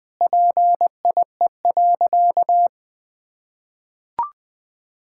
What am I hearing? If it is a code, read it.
Morse code, heard as a keyed tone, about 20 wpm, 700 Hz. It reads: PIE.
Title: Easy Sentences with Hints - 20wpm